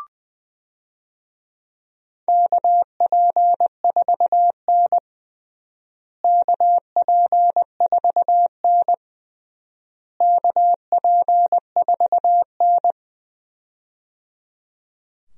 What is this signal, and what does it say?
Morse code, heard as a keyed tone, about 20 wpm, 700 Hz. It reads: KP4N KP4N KP4N
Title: DXCC-100 Call Signs - 4 Characters - Repeated 3x - 20wpm